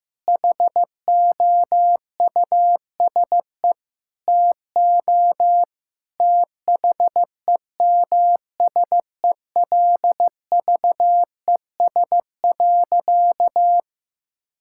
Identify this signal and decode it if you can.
Morse code, heard as a keyed tone, about 16 wpm, 700 Hz. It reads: HOUSE TO THEMSELVES.
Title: Sentences from Top 1000 Words - Review Part 5 - 15wpm